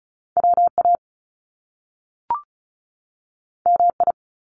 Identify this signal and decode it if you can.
Morse code, heard as a keyed tone, about 36 wpm, 700 Hz. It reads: WA MI
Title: US States 35wpm